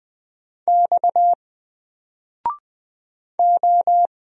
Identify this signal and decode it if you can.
Morse code, heard as a keyed tone, about 20 wpm, 700 Hz. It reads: X O
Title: Single Letter-Number Alt 20wpm